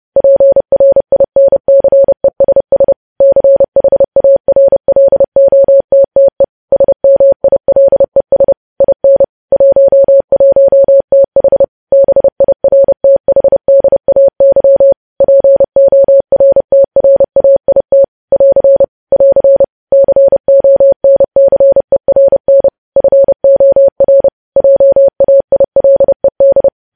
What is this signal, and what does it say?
Morse code, heard as a keyed tone, about 30 wpm, 550 Hz. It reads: PRINCESS CHARLOTTE SMILES IN 11TH BIRTHDAY PORTRAIT <AR> <AR> CONCERN FOR JAILED